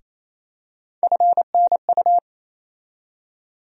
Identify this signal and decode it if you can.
Morse code, heard as a keyed tone, about 28 wpm, 700 Hz. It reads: FNU
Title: USA Call Sign Postfixes - 3 Letters 28wpm